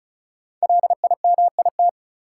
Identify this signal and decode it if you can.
Morse code, heard as a keyed tone, about 36 wpm, 700 Hz. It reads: LIMIT
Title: Sentences from Top 1500 Words - New Word Drill Alt - 35wpm